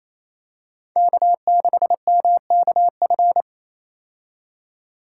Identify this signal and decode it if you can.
Morse code, heard as a keyed tone, about 28 wpm, 700 Hz. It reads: K6MKF
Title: K1USN SST QSO Practice - 28wpm